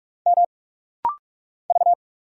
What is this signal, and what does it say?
Morse code, heard as a keyed tone, about 45 wpm, 700 Hz. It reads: M V
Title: Single Letter-Number Alt - Rapid-Fire 45wpm